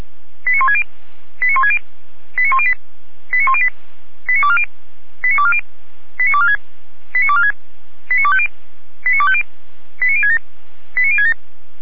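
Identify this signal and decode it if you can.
Five-tone selective call (ZVEI), heard as quick runs of five tones, tones of about 70 ms. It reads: 89160 89160 89108 89108 89240 89240 89246 89246 89250 89250 89067 89067
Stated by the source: ZVEI five-tone selective call received on an SDR.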